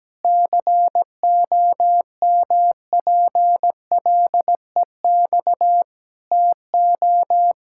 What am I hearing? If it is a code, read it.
Morse code, heard as a keyed tone, about 17 wpm, 700 Hz. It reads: COMPLEX TO
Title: Sentences from Top 1000 Words - Review Part 6 - 17wpm